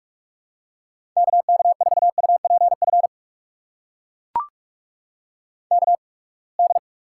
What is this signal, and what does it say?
Morse code, heard as a keyed tone, about 43 wpm, spaced as for 30 wpm, 700 Hz. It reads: KK4UPF KD
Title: USA Call Signs - 6 Characters - 8x Character Spacing - 45wpm